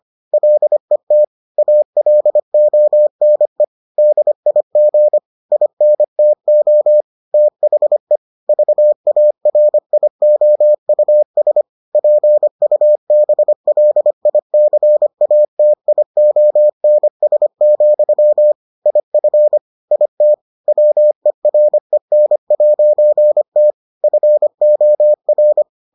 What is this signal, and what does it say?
Morse code, heard as a keyed tone, about 25 wpm, 600 Hz. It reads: LET ALONE DIG INTO THE VARIOUS PUBLICATIONS, IF IT WEREN'T FOR